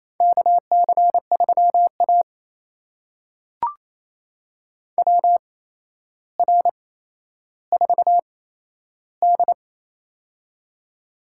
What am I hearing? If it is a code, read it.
Morse code, heard as a keyed tone, about 28 wpm, spaced as for 12 wpm, 700 Hz. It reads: KC3A WR4D
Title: USA Call Signs - 4 Characters - 8x Character Spacing - 28wpm